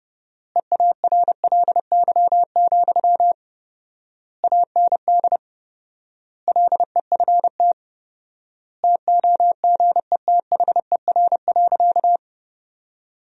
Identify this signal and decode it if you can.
Morse code, heard as a keyed tone, about 30 wpm, 700 Hz. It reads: EARLY, AND LEFT TOGETHER.
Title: 4x Word-Spacing - Sentences from Top 400 Words - 30wpm